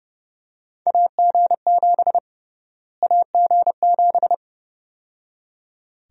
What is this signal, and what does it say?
Morse code, heard as a keyed tone, about 30 wpm, 700 Hz. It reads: AG7 AG7
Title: USA Call Sign Prefixes - Repeated 2x - 30wpm